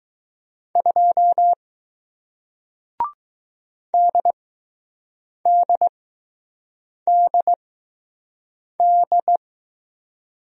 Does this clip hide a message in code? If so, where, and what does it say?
Yes: Morse code from 0.8 to 9.4 s, heard as a keyed tone, about 20 wpm, spaced as for 6 wpm, 700 Hz. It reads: 2 DDDD